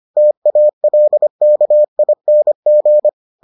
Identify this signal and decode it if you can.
Morse code, heard as a keyed tone, about 25 wpm, 600 Hz. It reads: TALKING